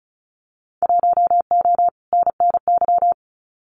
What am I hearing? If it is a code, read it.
Morse code, heard as a keyed tone, about 34 wpm, 700 Hz. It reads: 1O NNY
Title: Winter Field Day Exchanges 35wpm